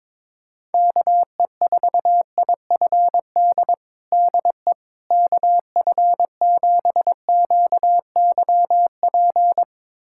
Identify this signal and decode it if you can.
Morse code, heard as a keyed tone, about 22 wpm, 700 Hz. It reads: KE4IFD DE KF7QYP
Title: USA Callsign DE Callsign - 6 Characters - 22wpm